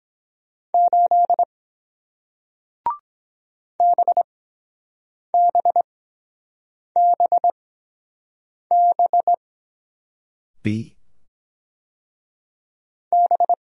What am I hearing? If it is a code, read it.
Morse code, heard as a keyed tone, about 23 wpm, spaced as for 7 wpm, 700 Hz. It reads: 8 BBBB B